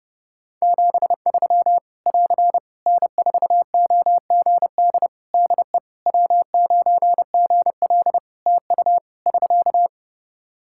Morse code, heard as a keyed tone, about 30 wpm, 700 Hz. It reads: 73 <AR> N4OGD DE W9GL TU <SK>
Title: K1USN SST QSO Practice 30wpm